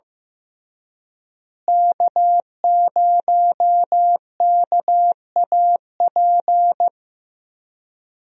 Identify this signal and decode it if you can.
Morse code, heard as a keyed tone, about 16 wpm, 700 Hz. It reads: K0KAP
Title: USA Call Signs - 5 Characters - Repeated 3x - 15wpm